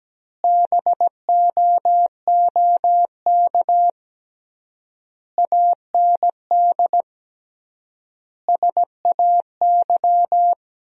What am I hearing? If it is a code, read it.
Morse code, heard as a keyed tone, about 17 wpm, 700 Hz. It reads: BOOK AND SAY
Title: 3x Word-Spacing - Sentences from Top 400 Words - 17wpm